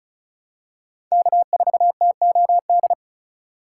Morse code, heard as a keyed tone, about 36 wpm, 700 Hz. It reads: K4TOD
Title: USA Call Signs - 5 Characters - 4x Character Spacing - 35wpm